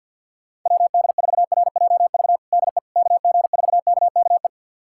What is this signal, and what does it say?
Morse code, heard as a keyed tone, about 52 wpm, 700 Hz. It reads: WD4RJV DE KG4KKE
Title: USA Callsign DE Callsign - 6 Characters - 50wpm